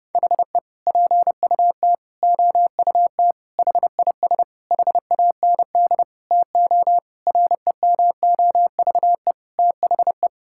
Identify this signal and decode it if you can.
Morse code, heard as a keyed tone, about 32 wpm, 700 Hz. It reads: HE PUT OUT HIS HAND TO REMOVE THE
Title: Sentences from Top 1000 Words - Review Part 5 - 30wpm